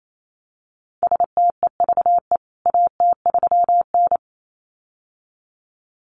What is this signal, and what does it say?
Morse code, heard as a keyed tone, about 28 wpm, 700 Hz. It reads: STEVE AT3N
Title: CWT Exchange 28wpm